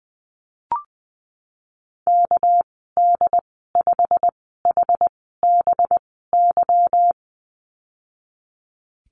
Morse code, heard as a keyed tone, about 20 wpm, spaced as for 15 wpm, 700 Hz. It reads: KD5HBY
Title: 238 Lesson 38g - Letter q - Call Signs 20wpm